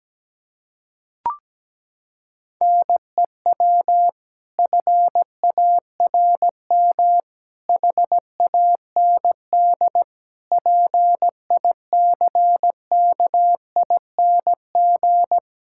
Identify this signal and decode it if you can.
Morse code, heard as a keyed tone, about 17 wpm, 700 Hz. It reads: NEW FARM HAND PICKING
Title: Pangram Challenge v2 - 17wpm